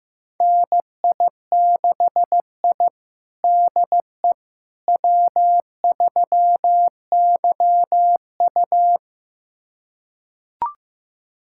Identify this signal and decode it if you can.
Morse code, heard as a keyed tone, about 15 wpm, 700 Hz. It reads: NI6I DE W3YU
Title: USA Callsign DE Callsign - 4 Characters - 15wpm